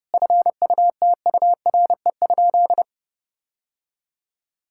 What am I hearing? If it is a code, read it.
Morse code, heard as a keyed tone, about 32 wpm, 700 Hz. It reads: FUTURE?
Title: Sentences from Top 1000 Words - Review Part 6 - 30wpm